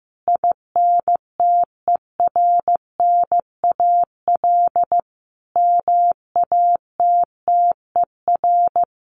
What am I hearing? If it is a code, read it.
Morse code, heard as a keyed tone, about 15 wpm, 700 Hz. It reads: INTERNAL MATTER
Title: Sentences from Top 1400 Words 15wpm